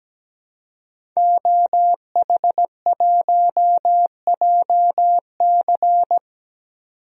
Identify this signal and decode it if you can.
Morse code, heard as a keyed tone, about 17 wpm, 700 Hz. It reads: OH1JC
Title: DXCC-100 Call Signs - 5 Characters - 17wpm